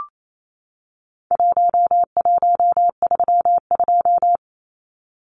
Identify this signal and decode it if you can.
Morse code, heard as a keyed tone, about 28 wpm, 700 Hz. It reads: 1132
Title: Numbers - Contest Style - 4-Digit 28wpm